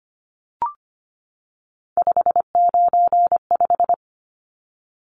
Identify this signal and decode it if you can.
Morse code, heard as a keyed tone, about 25 wpm, 700 Hz. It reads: 595
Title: Numbers - 3-Digit 25wpm